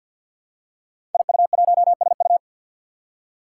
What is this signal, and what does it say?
Morse code, heard as a keyed tone, about 47 wpm, 700 Hz. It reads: IU1SU
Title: Five Challenging Letter-Number Combinations 50wpm